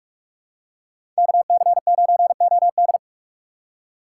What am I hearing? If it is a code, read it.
Morse code, heard as a keyed tone, about 44 wpm, 700 Hz. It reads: KC9OD
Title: USA Call Signs - 5 Characters - 3x Character Spacing - 45wpm